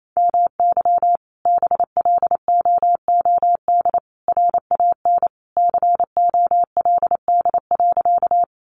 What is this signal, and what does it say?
Morse code, heard as a keyed tone, about 28 wpm, 700 Hz. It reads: MY BLOOD RAN COLD.
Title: Sentences from Top 1100 Words 28wpm